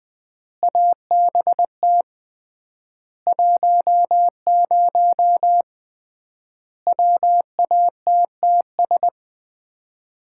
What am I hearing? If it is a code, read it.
Morse code, heard as a keyed tone, about 20 wpm, 700 Hz. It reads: ABT 10 WATTS
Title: Common QSO Phrases - 3x Word-Spacing - 20wpm